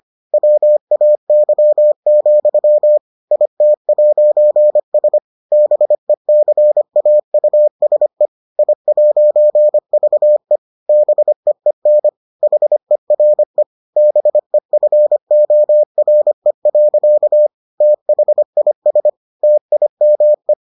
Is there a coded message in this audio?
Yes: Morse code from 0.3 to 20.5 s, heard as a keyed tone, about 25 wpm, 600 Hz. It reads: WAY, IT'S BECAUSE I'VE BEEN HERE BEFORE. THIS TIME